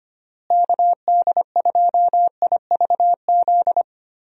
Morse code, heard as a keyed tone, about 25 wpm, 700 Hz. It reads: KD2IVZ
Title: USA Call Signs - 6 Characters - Repeated 2x - 25wpm